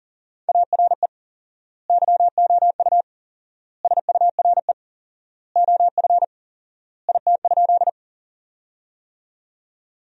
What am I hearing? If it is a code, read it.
Morse code, heard as a keyed tone, about 40 wpm, 700 Hz. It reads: ARE YOU SURE OF IT?
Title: 4x Word-Spacing - Sentences from Top 300 Words - 40wpm